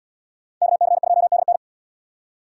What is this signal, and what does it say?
Morse code, heard as a keyed tone, about 66 wpm, 700 Hz. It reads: KC2DN